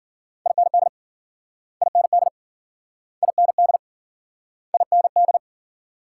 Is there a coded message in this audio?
Yes: Morse code from 0.5 to 5.4 s, heard as a keyed tone, about 52 wpm, 700 Hz. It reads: IND IND IND IND